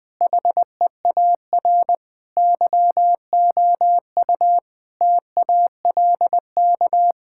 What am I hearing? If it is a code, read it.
Morse code, heard as a keyed tone, about 21 wpm, 700 Hz. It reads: HEAR YOU TALK
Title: Sentences from Top 1000 Words - Review Part 6 - 20wpm